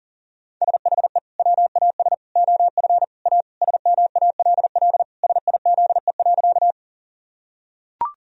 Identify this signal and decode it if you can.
Morse code, heard as a keyed tone, about 40 wpm, 700 Hz. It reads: SHE WAS OF A SMALL SIZE.
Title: Sentences from Top 1000 Words - Review Part 2 - 40wpm